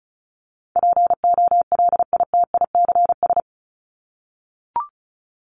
Morse code, heard as a keyed tone, about 34 wpm, 700 Hz. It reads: POLITICS
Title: Top 751-1000 Words 35wpm